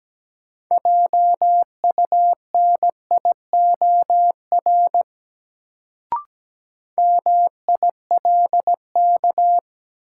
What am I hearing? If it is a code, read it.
Morse code, heard as a keyed tone, about 17 wpm, 700 Hz. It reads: JUNIOR MILK
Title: Sentences from Top 1500 Words - New Word Drill Alt - 17wpm